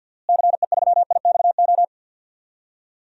Morse code, heard as a keyed tone, about 53 wpm, 700 Hz. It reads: CE3IXO
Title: DXCC-100 Call Signs - 6 Characters - Repeated 2x - 50wpm